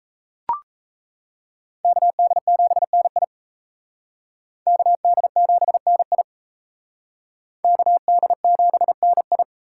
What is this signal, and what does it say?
Morse code, heard as a keyed tone, about 34 wpm, 700 Hz. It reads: KD7NI KD7NI KD7NI
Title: Speed Racing - USA 2x2 Call Signs 28wpm